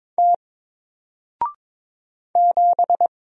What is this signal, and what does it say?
Morse code, heard as a keyed tone, about 22 wpm, 700 Hz. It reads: T 7
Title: Single Letter-Number Alt 22wpm